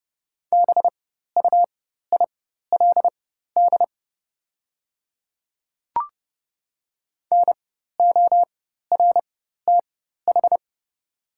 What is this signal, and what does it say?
Morse code, heard as a keyed tone, about 30 wpm, spaced as for 14 wpm, 690 Hz. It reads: BUILD NORTH
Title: 4x Character Spacing - Top 300 Words - 30wpm